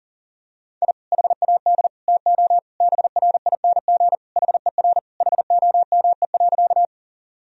Morse code, heard as a keyed tone, about 37 wpm, 700 Hz. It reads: I HAD TO BRING HER HOME.
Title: Sentences from Top 1000 Words - Review Part 3 - 40wpm